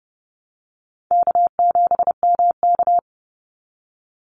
Morse code, heard as a keyed tone, about 30 wpm, 700 Hz. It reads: K7MK
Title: NA SOTA Activator Callsigns 30wpm